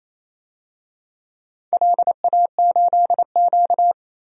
Morse code, heard as a keyed tone, about 28 wpm, 700 Hz. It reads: LA8Q
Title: DXCC-100 Call Signs - 4 Characters - 28wpm